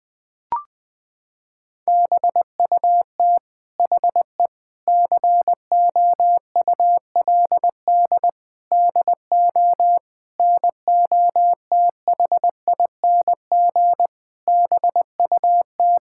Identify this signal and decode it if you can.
Morse code, heard as a keyed tone, about 19 wpm, 700 Hz. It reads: BUT HE COULD DO NOTHING BUT
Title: Sentences from Top 1600 Words 20wpm